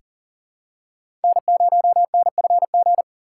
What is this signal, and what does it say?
Morse code, heard as a keyed tone, about 37 wpm, 700 Hz. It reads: N0NFG
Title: USA Call Signs - 5 Characters - 4x Character Spacing - 40wpm